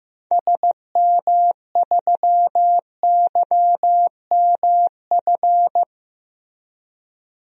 Morse code, heard as a keyed tone, about 15 wpm, 700 Hz. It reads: SM3YMF
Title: DXCC-100 Call Signs - 6 Characters - Repeated 2x - 15wpm